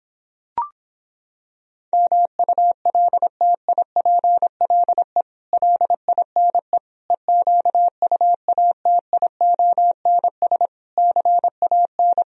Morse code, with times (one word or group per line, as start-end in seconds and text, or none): 1.9-5.2 MULTIPLE
5.5-6.8 LINE
7.1-10.7 EQUATIONS
11.0-12.2 CAN